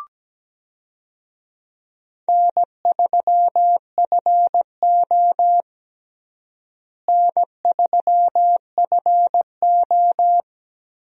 Morse code, heard as a keyed tone, about 17 wpm, 700 Hz. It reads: N3FO N3FO
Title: USA Call Signs - 4 Characters - Repeated 2x - 17wpm